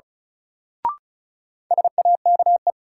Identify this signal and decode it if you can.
Morse code, heard as a keyed tone, about 36 wpm, 700 Hz. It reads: SAKE
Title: Sets of 4 Words - New Word Drill - 35wpm